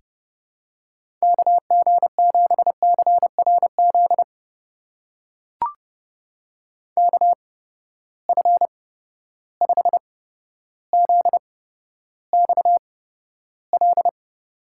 Morse code, heard as a keyed tone, about 30 wpm, spaced as for 13 wpm, 700 Hz. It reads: KG7CRZ KF5ZXL
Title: USA Call Signs - 6 Characters - 8x Character Spacing - 30wpm